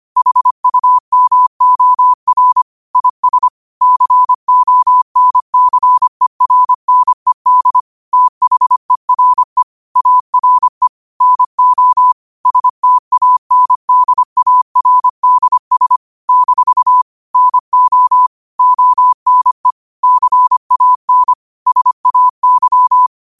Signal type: Morse code